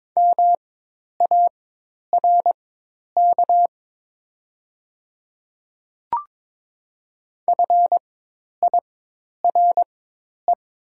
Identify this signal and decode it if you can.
Morse code, heard as a keyed tone, about 22 wpm, spaced as for 10 wpm, 700 Hz. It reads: MARK FIRE
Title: 4x Character Spacing - Top 500 Words - 22wpm